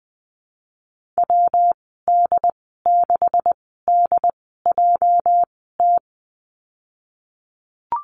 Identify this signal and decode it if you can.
Morse code, heard as a keyed tone, about 19 wpm, 700 Hz. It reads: W D 6 D J T
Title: 238 Lesson 38g - Letter q - Call Signs 20wpm